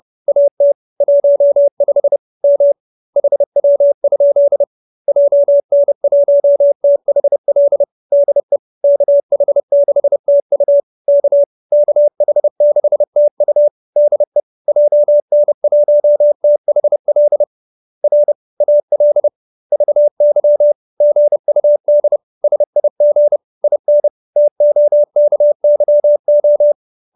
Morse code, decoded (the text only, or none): AT 15 M HW? JN1THL DE KH6TU K KH6TU DE JN1THL R AL VY GUD SIG IN TOKYO